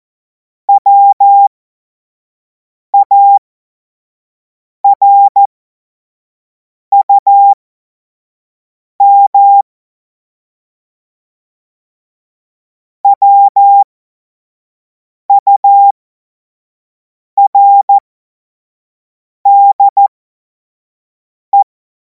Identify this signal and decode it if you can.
Morse code, heard as a keyed tone, about 14 wpm, spaced as for 5 wpm, 800 Hz. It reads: WARUM WURDE
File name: Morsekurs / Witze / Witz 25. August 2026 25 F70